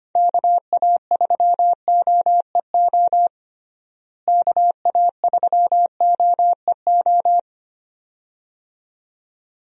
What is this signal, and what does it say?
Morse code, heard as a keyed tone, about 25 wpm, 700 Hz. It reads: KA3OEO KA3OEO
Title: USA Call Signs - 6 Characters - Repeated 2x - 25wpm